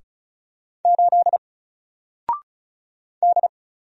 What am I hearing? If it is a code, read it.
Morse code, heard as a keyed tone, about 35 wpm, 700 Hz. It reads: 8 D
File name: Single-Letter-Number-35wpm